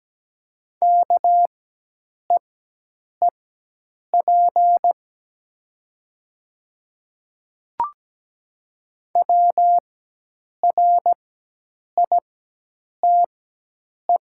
Morse code, heard as a keyed tone, about 17 wpm, spaced as for 8 wpm, 700 Hz. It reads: KEEP WRITE
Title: 4x Character Spacing - Top 500 Words - 17wpm